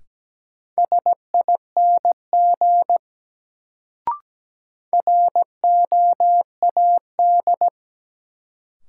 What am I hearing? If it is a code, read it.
Morse code, heard as a keyed tone, about 17 wpm, 700 Hz. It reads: SING ROAD
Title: Sentences from Top 400 Words - New Word Drill - 17wpm